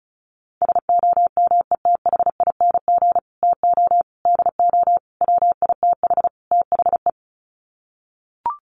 Morse code, heard as a keyed tone, about 34 wpm, 700 Hz. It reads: SOMETHING TO DO WITH THE
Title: Sets of 5 Words Encore 35wpm